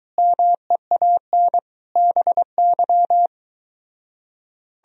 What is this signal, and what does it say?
Morse code, heard as a keyed tone, about 24 wpm, 700 Hz. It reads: MEAN BY